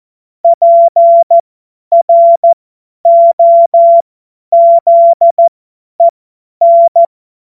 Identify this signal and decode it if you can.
Morse code, heard as a keyed tone, about 14 wpm, spaced as for 10 wpm, 680 Hz. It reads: PROZEN
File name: Morsekurs / Witze / Witz 26. April 2022 50 F70